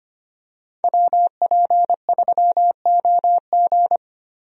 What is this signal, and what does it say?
Morse code, heard as a keyed tone, about 25 wpm, 700 Hz. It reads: WP3OG
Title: USA Call Signs - 5 Characters - 25wpm